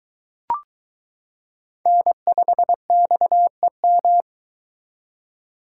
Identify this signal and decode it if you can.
Morse code, heard as a keyed tone, about 23 wpm, 700 Hz. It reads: N5XEM